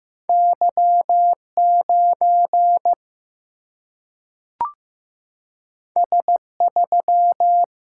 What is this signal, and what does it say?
Morse code, heard as a keyed tone, about 15 wpm, 700 Hz. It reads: Y 9 S 3
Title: Single-Letter Single-Number 15wpm